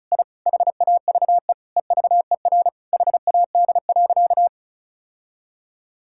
Morse code, heard as a keyed tone, about 36 wpm, 700 Hz. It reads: I HAVE EVER HAD.
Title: Sentences from Top 1000 Words - Review Part 6 - 35wpm